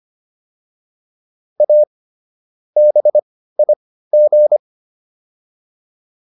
Morse code, heard as a keyed tone, about 25 wpm, spaced as for 15 wpm, 600 Hz. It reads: A BIG